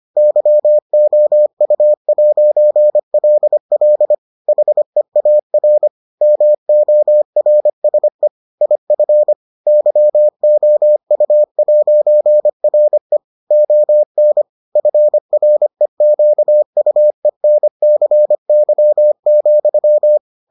Morse code, heard as a keyed tone, about 25 wpm, 600 Hz. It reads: YOU'LL HEAR MORSE IF YOU'RE ON FREQUENCY,